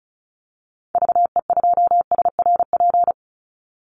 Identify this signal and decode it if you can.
Morse code, heard as a keyed tone, about 34 wpm, 700 Hz. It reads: VE2SRP
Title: Call Signs - 35wpm